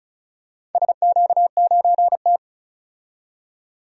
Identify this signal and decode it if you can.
Morse code, heard as a keyed tone, about 34 wpm, 700 Hz. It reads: SQ9T
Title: DXCC-100 Call Signs - 4 Characters - 35wpm